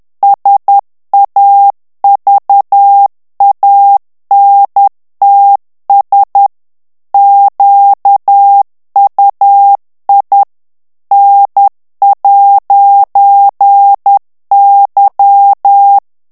Morse code, heard as a keyed tone, about 10 wpm, 800 Hz. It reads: SAVANTS QUI N'Y
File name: morse mp3 / FRANCAIS TEXTE CLAIR 2 VITESSE 10 MOTS MINUTES